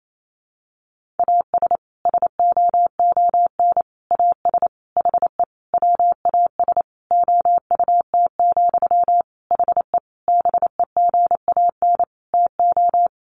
Morse code, heard as a keyed tone, about 28 wpm, 700 Hz. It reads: AS SOON AS HE WAS OUT, HE BEGAN TO
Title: Sentences from Top 1500 Words 28wpm